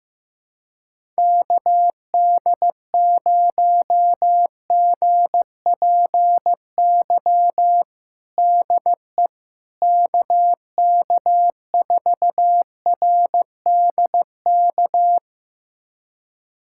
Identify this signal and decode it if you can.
Morse code, heard as a keyed tone, about 16 wpm, 700 Hz. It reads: KD0GPY DE KK4RDK
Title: USA Callsign DE Callsign 15wpm